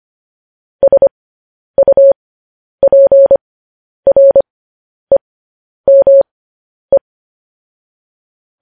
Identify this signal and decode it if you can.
Morse code, heard as a keyed tone, about 25 wpm, spaced as for 10 wpm, 550 Hz. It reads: SUPREME
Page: News Headlines in Morse Code at 10 WPM